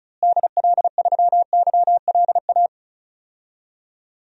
Morse code, heard as a keyed tone, about 34 wpm, 700 Hz. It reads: DL3YLA
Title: DXCC-100 Call Signs - 6 Characters - Repeated 2x - 35wpm